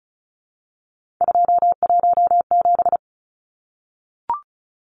Morse code, heard as a keyed tone, about 34 wpm, 700 Hz. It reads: 217
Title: Numbers - 3-Digit 35wpm